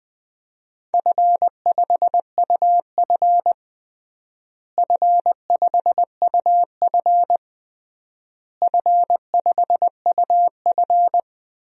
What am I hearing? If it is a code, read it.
Morse code, heard as a keyed tone, about 20 wpm, 700 Hz. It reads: F5UF F5UF F5UF
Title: DXCC-100 Call Signs - 4 Characters - Repeated 3x - 20wpm